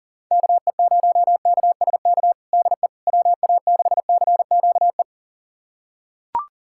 Morse code, heard as a keyed tone, about 40 wpm, 700 Hz. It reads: KE0KSK DE WA6CQE
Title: USA Callsign DE Callsign 40wpm